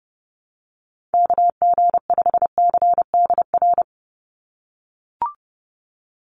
Morse code, heard as a keyed tone, about 30 wpm, 700 Hz. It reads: KG5CDR
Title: USA Call Signs 30wpm